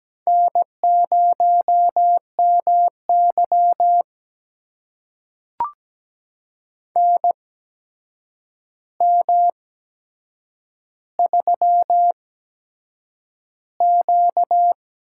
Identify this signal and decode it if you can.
Morse code, heard as a keyed tone, about 17 wpm, spaced as for 7 wpm, 700 Hz. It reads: N0MY NM3Q
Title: USA Call Signs - 4 Characters - 8x Character Spacing - 17wpm